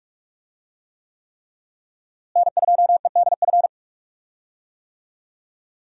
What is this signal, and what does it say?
Morse code, heard as a keyed tone, about 46 wpm, 700 Hz. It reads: N2EDF
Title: USA Call Signs - 5 Characters - Repeated 3x - 45wpm